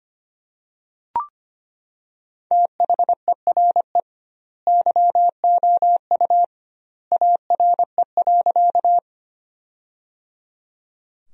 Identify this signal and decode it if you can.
Morse code, heard as a keyed tone, about 25 wpm, 700 Hz. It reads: THERE YOU ARE.
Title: 2x Word-Spacing - Sentences from Top 100 Words - 25wpm